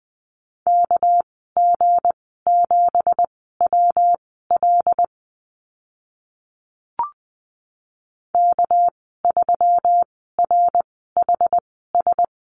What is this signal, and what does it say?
Morse code, heard as a keyed tone, about 20 wpm, spaced as for 14 wpm, 700 Hz. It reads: KG7WL K3RHS
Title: 207 Lesson 33g - Letter j - Call Signs 20wpm